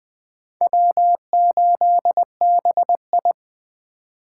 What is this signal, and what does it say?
Morse code, heard as a keyed tone, about 23 wpm, 700 Hz. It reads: W8BI
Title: Speed Racing - USA 1x2 Call Signs 20wpm